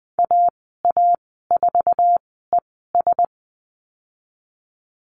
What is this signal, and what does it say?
Morse code, heard as a keyed tone, about 20 wpm, spaced as for 15 wpm, 700 Hz. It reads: AA4ES
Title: 082 Lesson 14g - Number 2 - Call Signs 20wpm